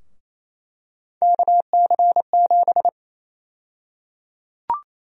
Morse code, heard as a keyed tone, about 28 wpm, 700 Hz. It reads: KC7
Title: USA Call Sign Prefixes - 28wpm